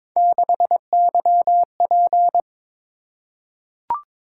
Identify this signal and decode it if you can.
Morse code, heard as a keyed tone, about 22 wpm, 700 Hz. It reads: 6YP
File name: practice / 22wpm / Three Letter-Number Combinations - 22wpm